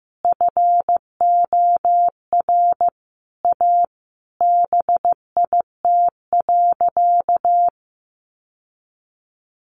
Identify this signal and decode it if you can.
Morse code, heard as a keyed tone, about 15 wpm, 700 Hz. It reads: FOR A BIT.
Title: Sentences from Top 900 Words 15wpm